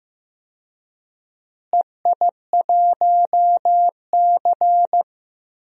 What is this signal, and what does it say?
Morse code, heard as a keyed tone, about 15 wpm, 700 Hz. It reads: EI1C
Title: DXCC-100 Call Signs - 4 Characters - Repeated 2x - 15wpm